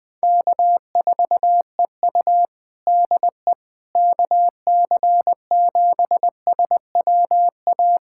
Morse code, heard as a keyed tone, about 19 wpm, 700 Hz. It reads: K4EU DE KC7SWA
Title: Practice QSO - Exchange 4 - Ending - 20wpm